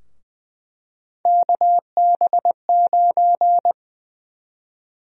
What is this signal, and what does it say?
Morse code, heard as a keyed tone, about 20 wpm, 700 Hz. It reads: KB9
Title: USA Call Sign Prefixes - 20wpm